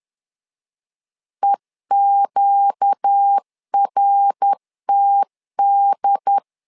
Morse code, heard as a keyed tone, about 10 wpm, 800 Hz. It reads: EQRTD
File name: LEZIONI / 4 Q 10wpm 2210